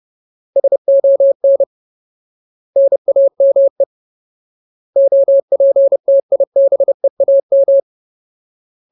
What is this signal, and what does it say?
Morse code, heard as a keyed tone, about 30 wpm, 550 Hz. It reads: SON NAME OPTIBEAM
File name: Ultimate Word List / 30WPM Ultimate